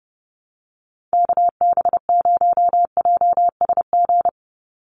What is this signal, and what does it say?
Morse code, heard as a keyed tone, about 30 wpm, 700 Hz. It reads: KB0JSG
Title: USA Call Signs 30wpm